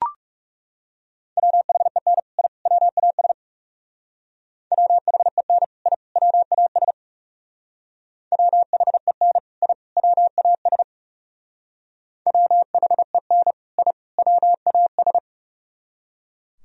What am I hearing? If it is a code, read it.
Morse code, heard as a keyed tone, about 38 wpm, 690 Hz. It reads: WHEN I WAS WHEN I WAS WHEN I WAS WHEN I WAS